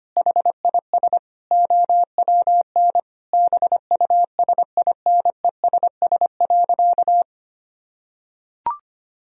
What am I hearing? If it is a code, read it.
Morse code, heard as a keyed tone, about 25 wpm, 700 Hz. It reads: HIS OWN BUSINESS.
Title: Sentences from Top 1000 Words - Review Part 2 - 25wpm